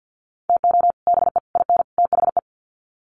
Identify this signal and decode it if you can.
Morse code, heard as a keyed tone, about 50 wpm, 700 Hz. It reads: TO BE IN THE
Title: Sets of 4 Words 50wpm